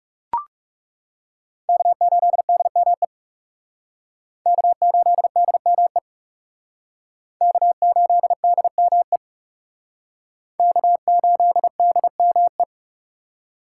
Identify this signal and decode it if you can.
Morse code, heard as a keyed tone, about 37 wpm, 700 Hz. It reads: K8DME K8DME K8DME K8DME
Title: Speed Racing - USA Call Signs 30wpm